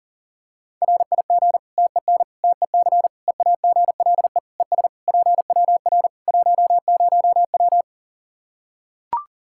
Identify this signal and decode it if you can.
Morse code, heard as a keyed tone, about 40 wpm, 700 Hz. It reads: RIG TEN TEC EAGLE ES PWR 10W
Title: K1USN SST QSO Practice 40wpm